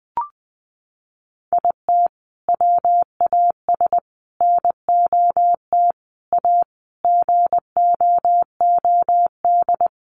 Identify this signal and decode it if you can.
Morse code, heard as a keyed tone, about 20 wpm, 700 Hz. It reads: IT WAS NOT A GOOD
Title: Sentences from Top 1100 Words 20wpm